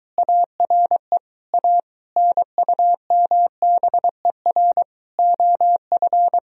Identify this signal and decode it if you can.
Morse code, heard as a keyed tone, about 22 wpm, 700 Hz. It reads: ARE A NUMBER OF